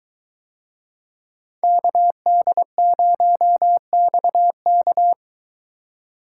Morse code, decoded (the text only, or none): KD0XK